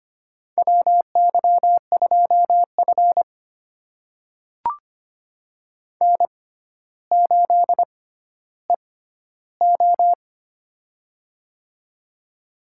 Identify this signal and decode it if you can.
Morse code, heard as a keyed tone, about 25 wpm, spaced as for 13 wpm, 700 Hz. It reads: WY2F N8EO
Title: USA Call Signs - 4 Characters - 6x Character Spacing - 25wpm